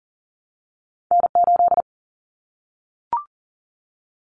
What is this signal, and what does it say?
Morse code, heard as a keyed tone, about 40 wpm, 700 Hz. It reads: N8
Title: Callsign Prefixes for Top 25 DXCC - Repeated 3x - 40wpm